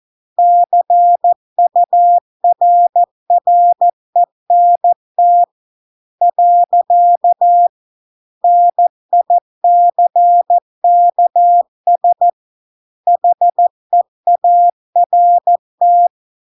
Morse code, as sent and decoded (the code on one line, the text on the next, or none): -.-. ..- .-. .-. -. - / .-.-.- / -. .. -.-. -.- ... / .... .- .-. -
CURRENT . NICKS HEART